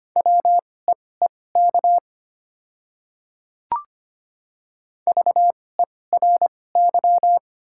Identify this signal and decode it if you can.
Morse code, heard as a keyed tone, about 25 wpm, spaced as for 18 wpm, 700 Hz. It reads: WEEK VERY
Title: 2x Character Spacing - Top 500 Words - 25wpm